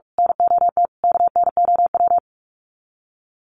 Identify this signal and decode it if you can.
Morse code, heard as a keyed tone, about 44 wpm, 700 Hz. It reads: NOT KNOW
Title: Sets of 4 Words 45wpm